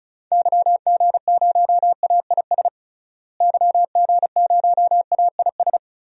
Morse code, heard as a keyed tone, about 34 wpm, 700 Hz. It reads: YG0AIS YG0AIS
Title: DXCC-100 Call Signs - 6 Characters - Repeated 3x - 35wpm